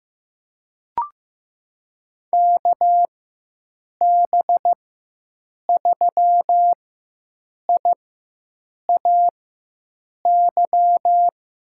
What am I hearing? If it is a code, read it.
Morse code, heard as a keyed tone, about 15 wpm, spaced as for 7 wpm, 700 Hz. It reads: KB3IAY
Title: USA Call Signs - 6 Characters - 4x Character Spacing - 15wpm